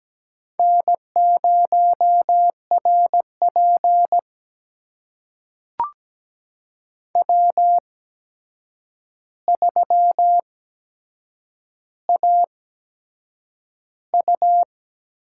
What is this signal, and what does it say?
Morse code, heard as a keyed tone, about 17 wpm, spaced as for 7 wpm, 700 Hz. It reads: N0RP W3AU
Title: USA Call Signs - 4 Characters - 8x Character Spacing - 17wpm